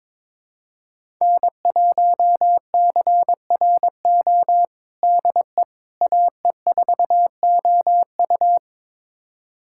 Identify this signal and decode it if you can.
Morse code, heard as a keyed tone, about 22 wpm, 700 Hz. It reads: N1CRO DE AE4OU
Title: USA Callsign DE Callsign - 5 Characters - 22wpm